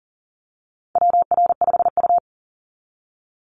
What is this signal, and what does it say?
Morse code, heard as a keyed tone, about 37 wpm, 700 Hz. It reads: WR5U
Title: Call Signs Repeated 2x - 40wpm